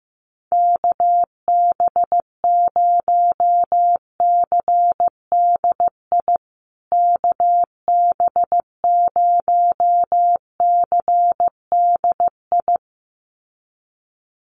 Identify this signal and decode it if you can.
Morse code, heard as a keyed tone, about 15 wpm, 700 Hz. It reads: KB0CDI KB0CDI
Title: USA Call Signs Repeated 2x 15wpm